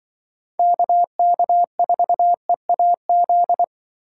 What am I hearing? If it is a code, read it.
Morse code, heard as a keyed tone, about 25 wpm, 700 Hz. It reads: KK4EAZ
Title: Speed Racing - USA Call Signs 20wpm